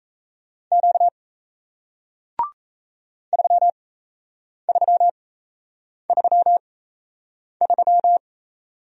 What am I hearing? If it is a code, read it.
Morse code, heard as a keyed tone, about 37 wpm, spaced as for 8 wpm, 700 Hz. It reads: Q 3333